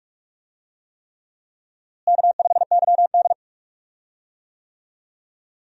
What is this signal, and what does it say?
Morse code, heard as a keyed tone, about 46 wpm, 700 Hz. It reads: K5YD
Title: USA Call Signs - 4 Characters - Repeated 3x - 45wpm